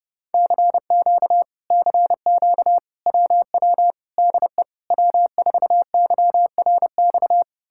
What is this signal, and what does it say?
Morse code, heard as a keyed tone, about 33 wpm, 700 Hz. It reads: CQ CQ WW DE W4YRX